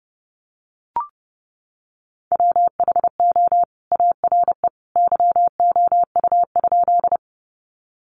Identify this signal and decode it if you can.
Morse code, heard as a keyed tone, about 30 wpm, 700 Hz. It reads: WHO ARE YOU?
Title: Sentences from Top 100 Words 30wpm